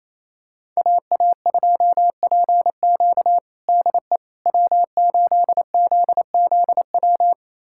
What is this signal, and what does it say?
Morse code, heard as a keyed tone, about 28 wpm, 700 Hz. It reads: AA2PQ DE W8ZZW
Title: USA Callsign DE Callsign 28wpm